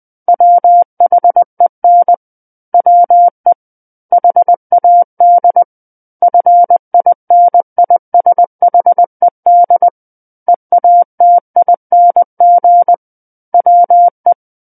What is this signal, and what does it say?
Morse code, heard as a keyed tone, about 20 wpm, 700 Hz. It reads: WHEN WE HAD FINISHED EATING WE
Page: SKCC Learning Center - Morse Code Practice Files